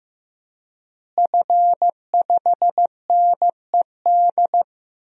Morse code, heard as a keyed tone, about 15 wpm, 700 Hz. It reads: F5NED
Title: DXCC-100 Call Signs - 5 Characters - Repeated 3x - 15wpm